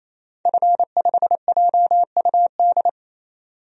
Morse code, heard as a keyed tone, about 28 wpm, 700 Hz. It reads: F5JUD
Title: DXCC-100 Call Signs - 5 Characters - 28wpm